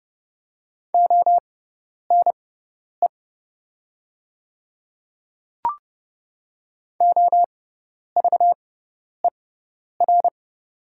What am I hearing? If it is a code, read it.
Morse code, heard as a keyed tone, about 30 wpm, spaced as for 10 wpm, 700 Hz. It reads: ONE OVER
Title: 6x Character Spacing - Top 300 Words - 30wpm